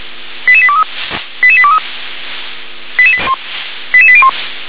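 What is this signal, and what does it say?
Five-tone selective call (ZVEI), heard as quick runs of five tones, tones of about 70 ms. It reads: 88823 88823 88801
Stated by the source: ZVEI five-tone selective call received on SDR.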